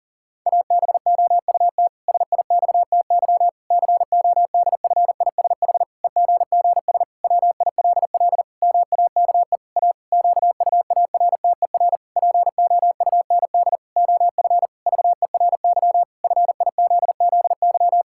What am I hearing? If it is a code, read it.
Morse code, heard as a keyed tone, about 40 wpm, 700 Hz. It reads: ABOUT SIXTY CODFISH EGGS WILL MAKE A QUARTER POUND OF VERY FIZZY
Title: Pangram Challenge v2 - 40wpm